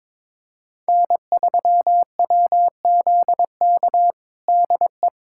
Morse code, heard as a keyed tone, about 22 wpm, 700 Hz. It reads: N3WZK DE
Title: USA Callsign DE Callsign 22wpm